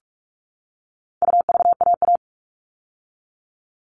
Morse code, heard as a keyed tone, about 44 wpm, 700 Hz. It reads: UVAA
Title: All Cut Numbers - 4-Digit 45wpm